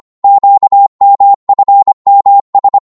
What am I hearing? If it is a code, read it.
Morse code, heard as a keyed tone, about 24 wpm, 800 Hz. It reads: QMFMS
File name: Morse / ltr-25wpm